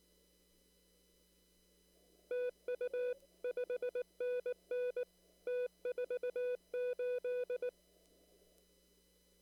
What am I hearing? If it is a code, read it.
Morse code, heard as a keyed tone, about 19 wpm, 500 Hz. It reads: TU 5NN T48
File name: CW / kt5j wpx cw-op-k5pi